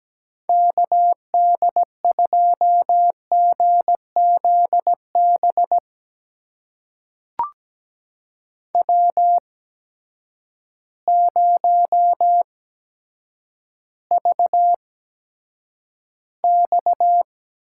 Morse code, heard as a keyed tone, about 17 wpm, spaced as for 8 wpm, 700 Hz. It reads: KD2GZB W0VX
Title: USA Call Signs - 8x Character Spacing - 17wpm